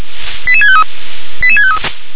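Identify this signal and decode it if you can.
Five-tone selective call (ZVEI), heard as quick runs of five tones, tones of about 70 ms. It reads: 88653 88653
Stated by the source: ZVEI five-tone selective call 88654 received by SDR.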